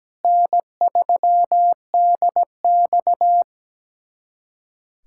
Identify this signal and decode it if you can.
Morse code, heard as a keyed tone, about 17 wpm, 700 Hz. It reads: N3DX
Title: USA Callsign DE Callsign - 4 Characters - 17wpm